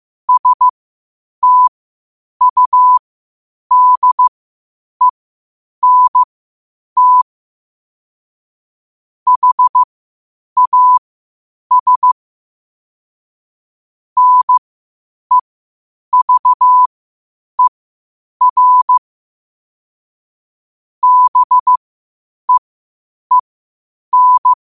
Morse code, heard as a keyed tone, about 15 wpm, spaced as for 8 wpm, 1000 Hz. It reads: STUDENT HAS NEVER BEEN
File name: qotdfiles / 8wpm